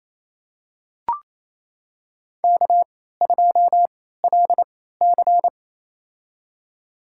Morse code, heard as a keyed tone, about 28 wpm, spaced as for 16 wpm, 700 Hz. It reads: K2LC
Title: USA Call Signs - 4 Characters - 3x Character Spacing - 28wpm